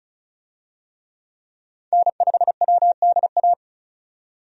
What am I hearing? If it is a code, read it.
Morse code, heard as a keyed tone, about 36 wpm, 700 Hz. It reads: N5WDA
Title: USA Call Signs - 5 Characters - 35wpm